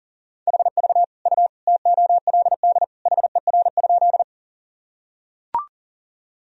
Morse code, heard as a keyed tone, about 40 wpm, 700 Hz. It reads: HV U TOLD HER?
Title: Sentences from Top 400 Words and Abbreviations 40wpm